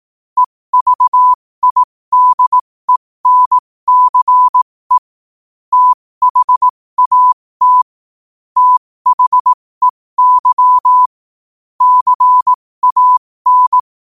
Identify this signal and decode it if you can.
Morse code, heard as a keyed tone, about 18 wpm, spaced as for 15 wpm, 1000 Hz. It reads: EVIDENCE THAT THEY CAN
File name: qotdfiles / 15wpm